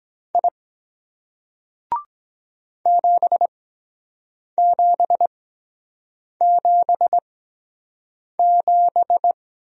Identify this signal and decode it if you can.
Morse code, heard as a keyed tone, about 23 wpm, spaced as for 7 wpm, 700 Hz. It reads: I 7777